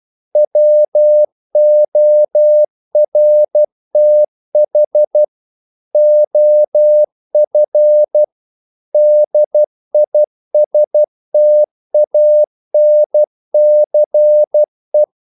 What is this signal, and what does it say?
Morse code, heard as a keyed tone, about 12 wpm, 600 Hz. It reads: WORTH OF DISTANCE
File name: CW / Practice / If12